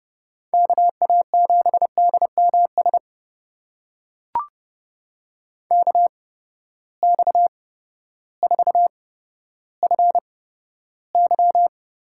Morse code, heard as a keyed tone, about 30 wpm, spaced as for 14 wpm, 700 Hz. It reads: KA7DMS KX4FY
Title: USA Call Signs - 8x Character Spacing - 30wpm